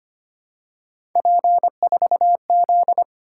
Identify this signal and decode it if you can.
Morse code, heard as a keyed tone, about 29 wpm, 700 Hz. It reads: P4Z
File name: Speed-Racing - Three Letter-Number Combinations - 25wpm